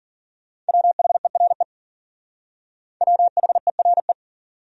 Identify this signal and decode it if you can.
Morse code, heard as a keyed tone, about 50 wpm, 700 Hz. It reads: WHERE WHERE